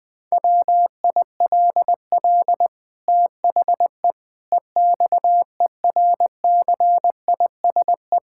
Morse code, heard as a keyed tone, about 21 wpm, 700 Hz. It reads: WILL THE EXERCISE
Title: Sentences from Top 1000 Words - Review Part 6 - 20wpm